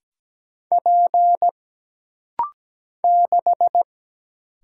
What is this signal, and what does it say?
Morse code, heard as a keyed tone, about 17 wpm, 700 Hz. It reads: P 6
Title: Single Letter-Number - Rapid-Fire 17wpm